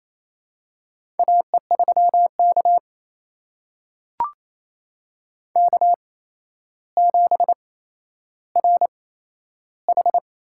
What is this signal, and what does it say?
Morse code, heard as a keyed tone, about 28 wpm, spaced as for 12 wpm, 700 Hz. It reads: AE3K K7RH
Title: USA Call Signs - 4 Characters - 8x Character Spacing - 28wpm